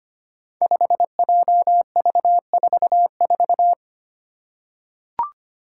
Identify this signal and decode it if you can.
Morse code, heard as a keyed tone, about 25 wpm, 700 Hz. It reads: 5JV44
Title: Five Challenging Letter-Number Combinations 25wpm